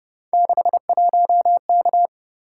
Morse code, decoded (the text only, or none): 61K